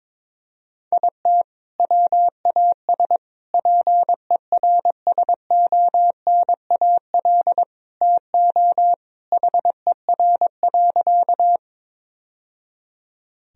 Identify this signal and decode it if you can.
Morse code, heard as a keyed tone, about 22 wpm, 700 Hz. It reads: IT WAS PERSONAL TO HER.
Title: Sentences from Top 1000 Words - Review Part 4 - 22wpm